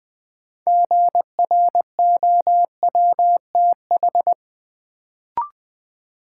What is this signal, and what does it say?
Morse code, heard as a keyed tone, about 20 wpm, 700 Hz. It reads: GROWTH
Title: Sentences from Top 1300 Words - New Word Drill Alt - 20wpm